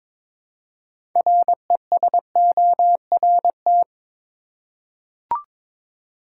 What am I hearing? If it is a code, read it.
Morse code, heard as a keyed tone, about 22 wpm, 700 Hz. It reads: RESORT
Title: Six Letter Words - 22wpm